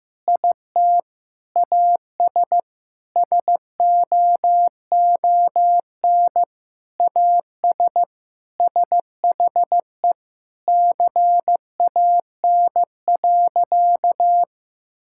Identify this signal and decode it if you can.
Morse code, heard as a keyed tone, about 15 wpm, 700 Hz. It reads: IT AS SOON AS SHE CAN.
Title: Sentences from Top 1000 Words - Review Part 5 - 15wpm